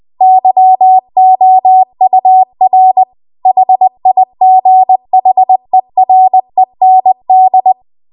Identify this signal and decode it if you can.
Morse code, heard as a keyed tone, about 20 wpm, 750 Hz. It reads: YOUR HIGHEREND